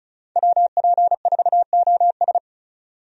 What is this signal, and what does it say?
Morse code, heard as a keyed tone, about 34 wpm, 700 Hz. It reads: WP4OS
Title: USA Call Signs - 5 Characters - 35wpm